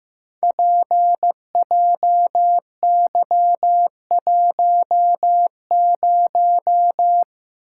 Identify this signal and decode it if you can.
Morse code, heard as a keyed tone, about 15 wpm, 700 Hz. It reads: PJY10
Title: Code Groups - 5 Letter:Numbers - 15wpm